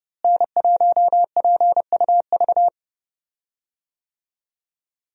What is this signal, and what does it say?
Morse code, heard as a keyed tone, about 31 wpm, 700 Hz. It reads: N1PUV
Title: USA Call Signs - 5 Characters - Repeated 2x - 30wpm